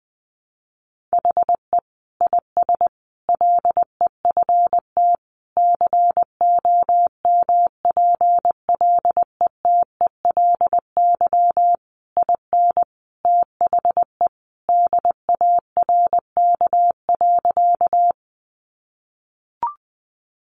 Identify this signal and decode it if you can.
Morse code, heard as a keyed tone, about 20 wpm, 700 Hz. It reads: HE IS LEFT COMPLETELY IN THE DARK.
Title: Sentences from Top 1100 Words 20wpm